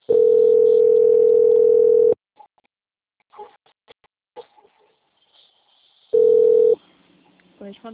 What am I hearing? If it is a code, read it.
Ringback tone (North America), 440 + 480 Hz.
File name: out- - - - - .51757